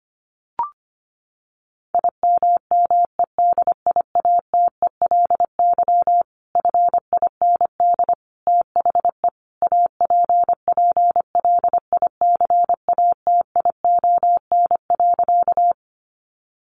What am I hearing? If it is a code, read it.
Morse code, heard as a keyed tone, about 25 wpm, 700 Hz. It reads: IMMEDIATELY FIND THE APPLICATION.
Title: Sentences from Top 1200 Words 25wpm